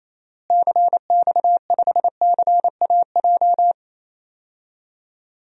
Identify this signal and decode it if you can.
Morse code, heard as a keyed tone, about 27 wpm, 700 Hz. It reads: CX5CAJ
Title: DXCC-100 Call Signs - 6 Characters - Repeated 2x - 28wpm